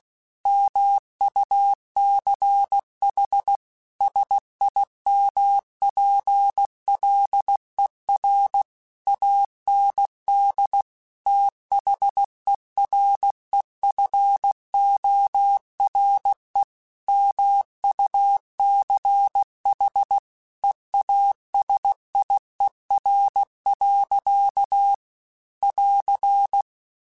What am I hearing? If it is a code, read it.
Morse code, heard as a keyed tone, about 16 wpm, 800 Hz. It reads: MUCH SIMPLER AND THEREFORE MUCH EASIER. <AR>
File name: Technical Subjects / Technical Subjects-mp3s / AND Gate @15